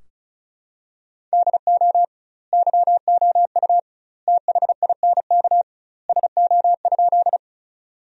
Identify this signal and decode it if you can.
Morse code, heard as a keyed tone, about 36 wpm, 700 Hz. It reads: DO YOU THINK SO?
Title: 2x Word-Spacing - Sentences from Top 200 Words - 35wpm